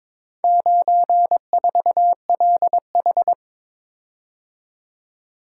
Morse code, heard as a keyed tone, about 21 wpm, 700 Hz. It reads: 94LH
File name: practice / 22wpm / Four Letter-Number Combinations - 22wpm